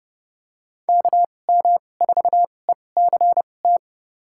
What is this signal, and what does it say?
Morse code, heard as a keyed tone, about 30 wpm, spaced as for 22 wpm, 700 Hz. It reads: KM4ECT
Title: USA Call Signs - 6 Characters - 2x Character Spacing - 30wpm